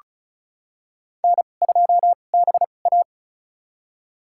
Morse code, heard as a keyed tone, about 34 wpm, spaced as for 25 wpm, 700 Hz. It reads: N2BA
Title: USA Call Signs - 4 Characters - 2x Character Spacing - 35wpm